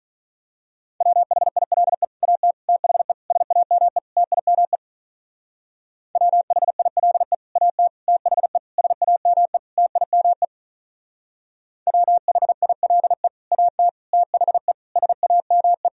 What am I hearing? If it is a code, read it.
Morse code, heard as a keyed tone, about 42 wpm, 700 Hz. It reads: WHILE AT THE SAME TIME WHILE AT THE SAME TIME WHILE AT THE SAME
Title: Speed-Racing - Sets of 5 Words 35wpm